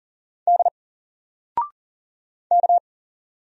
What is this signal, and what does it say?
Morse code, heard as a keyed tone, about 39 wpm, 700 Hz. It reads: D K
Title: Single Letter-Number 40wpm